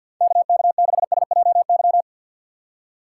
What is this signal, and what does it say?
Morse code, heard as a keyed tone, about 52 wpm, 700 Hz. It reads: KK6SJY
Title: USA Call Signs - 6 Characters - 3x Character Spacing - 50wpm